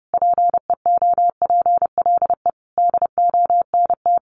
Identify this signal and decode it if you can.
Morse code, heard as a keyed tone, about 30 wpm, 700 Hz. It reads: PEOPLE DONT
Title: Sentences from Top 600 Words 30wpm